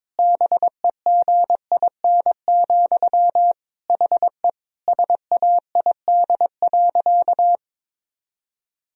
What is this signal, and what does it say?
Morse code, heard as a keyed tone, about 22 wpm, 700 Hz. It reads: BEGIN, HE SAID.
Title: Sentences from Top 1000 Words - Review Part 1 - 22wpm